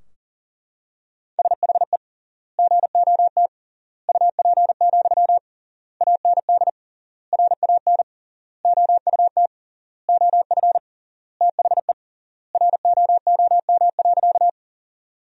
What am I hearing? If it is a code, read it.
Morse code, heard as a keyed tone, about 40 wpm, 700 Hz. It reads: SHE GOT UP, AND RAN OUT OF THE ROOM.
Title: 3x Word-Spacing - Sentences from Top 500 Words - 40wpm